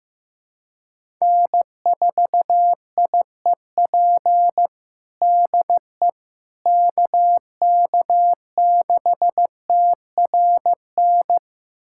Morse code, heard as a keyed tone, about 15 wpm, 700 Hz. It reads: N4IEP DE KK6TRN
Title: USA Callsign DE Callsign 15wpm